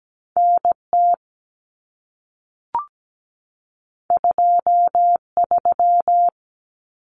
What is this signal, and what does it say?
Morse code, heard as a keyed tone, about 17 wpm, 700 Hz. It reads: NT 23
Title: Common Cut Numbers - 2-Digit 17wpm